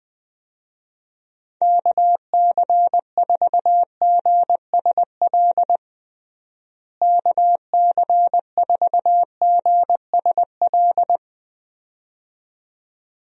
Morse code, heard as a keyed tone, about 20 wpm, 700 Hz. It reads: KC4GSL KC4GSL
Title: USA Call Signs - 6 Characters - Repeated 2x - 20wpm